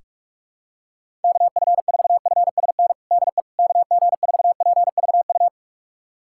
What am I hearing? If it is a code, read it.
Morse code, heard as a keyed tone, about 41 wpm, 700 Hz. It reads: KF4FSN DE KG4PVU
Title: USA Callsign DE Callsign - 6 Characters - 45wpm